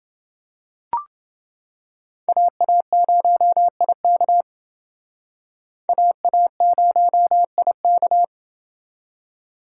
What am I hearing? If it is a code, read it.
Morse code, heard as a keyed tone, about 29 wpm, 700 Hz. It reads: AA0IK AA0IK